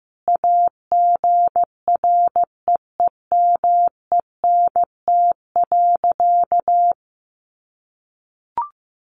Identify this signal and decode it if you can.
Morse code, heard as a keyed tone, about 14 wpm, 700 Hz. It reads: AGREEMENT.
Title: Sentences from Top 1500 Words 15wpm